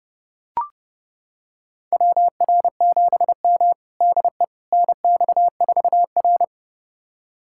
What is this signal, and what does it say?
Morse code, heard as a keyed tone, about 30 wpm, 700 Hz. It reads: WR7M DE NX4R
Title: USA Callsign DE Callsign - 4 Characters - 30wpm